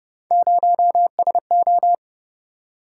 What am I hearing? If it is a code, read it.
Morse code, heard as a keyed tone, about 25 wpm, 700 Hz. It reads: 0SO